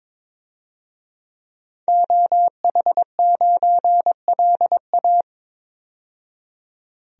Morse code, heard as a keyed tone, about 22 wpm, 700 Hz. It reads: OH9LA
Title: DXCC-100 Call Signs - 5 Characters - 22wpm